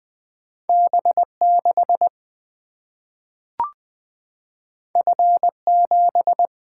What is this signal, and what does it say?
Morse code, heard as a keyed tone, about 20 wpm, 700 Hz. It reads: B 6 F 7
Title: Single-Letter Single-Number 20wpm